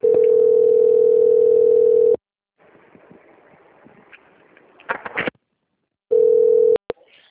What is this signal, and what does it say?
Ringback tone (North America), 440 + 480 Hz.
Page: Limeira - SP